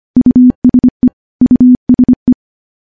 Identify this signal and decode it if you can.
Morse code, heard as a keyed tone, about 25 wpm, 250 Hz. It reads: USE USE
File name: morse-training-25WPM